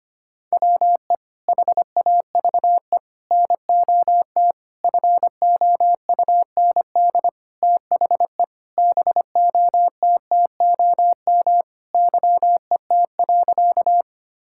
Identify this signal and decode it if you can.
Morse code, heard as a keyed tone, about 25 wpm, 700 Hz. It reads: WE HAVE NOT FOUND THE BOTTOM YET.
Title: Sentences from Top 1000 Words - Review Part 6 - 25wpm